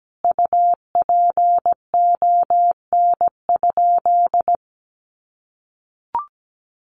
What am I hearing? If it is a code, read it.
Morse code, heard as a keyed tone, about 17 wpm, 700 Hz. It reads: UPON?
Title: Sentences from Top 1300 Words 17wpm